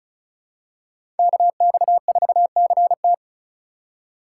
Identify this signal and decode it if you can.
Morse code, heard as a keyed tone, about 36 wpm, 700 Hz. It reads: KX4CT
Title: USA Call Signs - 5 Characters - 2x Character Spacing - 35wpm